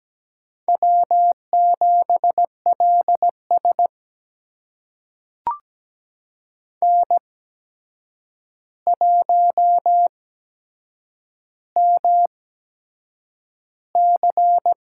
Morse code, heard as a keyed tone, about 17 wpm, spaced as for 7 wpm, 700 Hz. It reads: W7LS N1MC
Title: USA Call Signs - 4 Characters - 8x Character Spacing - 17wpm